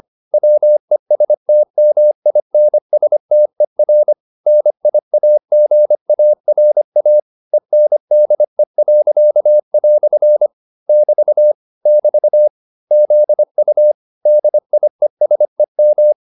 Morse code, heard as a keyed tone, about 25 wpm, 600 Hz. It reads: WESTMINSTER NIAGARA ENDE." <BT> <BT> ZU DIESEM